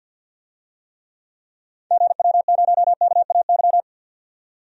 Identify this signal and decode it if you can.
Morse code, heard as a keyed tone, about 52 wpm, 700 Hz. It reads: GW0KAY
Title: DXCC-100 Call Signs - 6 Characters - 50wpm